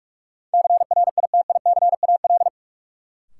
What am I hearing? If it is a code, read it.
Morse code, heard as a keyed tone, about 45 wpm, 700 Hz. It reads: CRITICAL
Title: Sentences from Top 1100 Words - New Word Drill - 45wpm